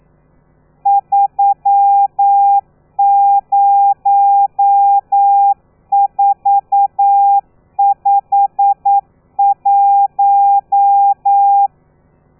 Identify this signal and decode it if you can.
Morse code, heard as a keyed tone, about 9 wpm, 800 Hz. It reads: 30451
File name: telegrafia / i2tfj / Lez12 40